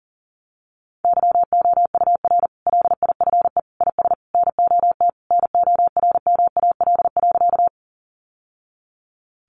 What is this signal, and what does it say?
Morse code, heard as a keyed tone, about 40 wpm, 700 Hz. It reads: YOUR LIFE IS NOT NORMAL.
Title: Sentences from Top 1200 Words 40wpm